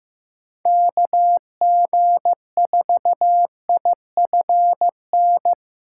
Morse code, heard as a keyed tone, about 15 wpm, 700 Hz. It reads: KG4IFN
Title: USA Call Signs - 3x Character Spacing - 15wpm